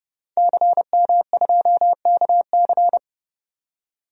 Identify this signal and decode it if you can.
Morse code, heard as a keyed tone, about 32 wpm, 700 Hz. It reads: CM2KC
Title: DXCC-100 Call Signs - 5 Characters - 30wpm